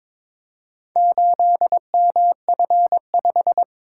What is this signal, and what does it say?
Morse code, heard as a keyed tone, about 22 wpm, 700 Hz. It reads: 8MF5
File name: practice / 22wpm / Four Letter-Number Combinations - 22wpm